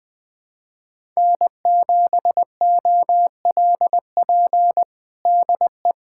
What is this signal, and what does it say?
Morse code, heard as a keyed tone, about 21 wpm, 700 Hz. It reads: N7OLP DE
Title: USA Callsign DE Callsign - 5 Characters - 20wpm